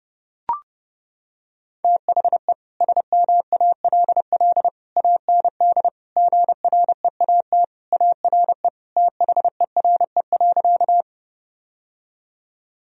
Morse code, heard as a keyed tone, about 30 wpm, 700 Hz. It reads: THE SMALL AND GREAT ARE THERE.
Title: Sentences from Top 1000 Words - Review Part 1 - 30wpm